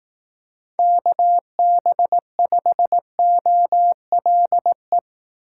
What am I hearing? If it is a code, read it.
Morse code, heard as a keyed tone, about 18 wpm, 700 Hz. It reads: KB5OLE
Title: Speed Racing - USA 2x3 Call Signs 15wpm